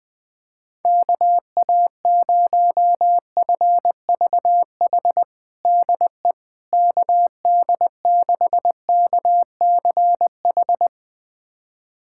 Morse code, heard as a keyed tone, about 20 wpm, 700 Hz. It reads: KA0FVH DE KD6KCH
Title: USA Callsign DE Callsign - 6 Characters - 20wpm